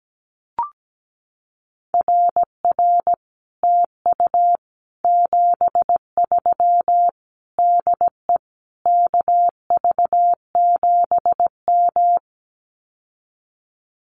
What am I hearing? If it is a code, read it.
Morse code, heard as a keyed tone, about 17 wpm, 700 Hz. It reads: RR TU 73 DE KV7M
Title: Common QSO Phrases Encore 17wpm